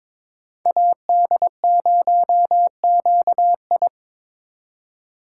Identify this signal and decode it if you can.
Morse code, heard as a keyed tone, about 23 wpm, 700 Hz. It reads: AD0QI
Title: USA Call Signs - 2x Character Spacing - 22wpm